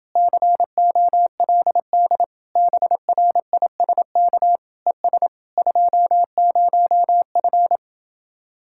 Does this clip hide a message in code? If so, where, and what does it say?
Yes: Morse code from 0.2 to 7.8 s, heard as a keyed tone, about 30 wpm, 700 Hz. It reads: COLD BRISK ES 20F